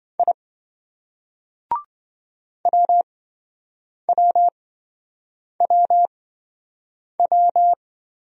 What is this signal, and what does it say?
Morse code, heard as a keyed tone, about 26 wpm, spaced as for 7 wpm, 700 Hz. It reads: I WWWW